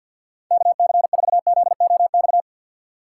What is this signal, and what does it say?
Morse code, heard as a keyed tone, about 49 wpm, 700 Hz. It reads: KC4ZOX
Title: USA Call Signs - 6x Character Spacing - 50wpm